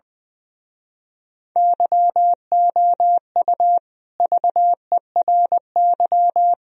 Morse code, heard as a keyed tone, about 20 wpm, 700 Hz. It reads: YOU VERY
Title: Speed-Racing - Sets of 4 Words 20wpm